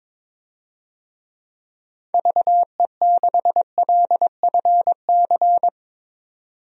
Morse code, heard as a keyed tone, about 22 wpm, 700 Hz. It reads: VE6LFC
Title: DXCC-100 Call Signs - 6 Characters - 22wpm